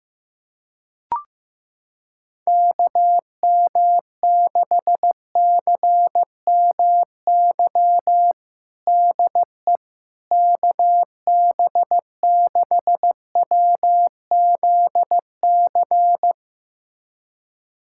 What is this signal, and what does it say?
Morse code, heard as a keyed tone, about 16 wpm, 700 Hz. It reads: KM6CMY DE KB6WZC
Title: USA Callsign DE Callsign - 6 Characters - 15wpm